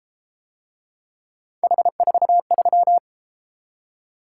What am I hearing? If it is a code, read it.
Morse code, heard as a keyed tone, about 31 wpm, 700 Hz. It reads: H43